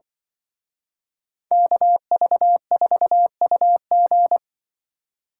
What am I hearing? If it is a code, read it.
Morse code, heard as a keyed tone, about 24 wpm, 700 Hz. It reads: KV4UG
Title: Speed Racing - USA 2x2 Call Signs 20wpm